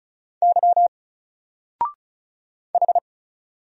Morse code, heard as a keyed tone, about 33 wpm, 700 Hz. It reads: Y H
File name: Single-Letter-Number-35wpm